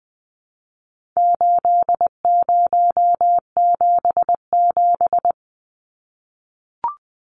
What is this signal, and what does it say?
Morse code, heard as a keyed tone, about 20 wpm, 700 Hz. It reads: 8077
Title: Numbers - 4-Digit 20wpm